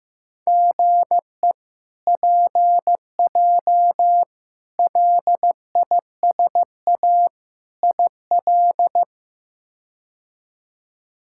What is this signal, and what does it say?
Morse code, heard as a keyed tone, about 15 wpm, 700 Hz. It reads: GE PJ LISA IL
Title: K1USN SST QSO Practice - 15wpm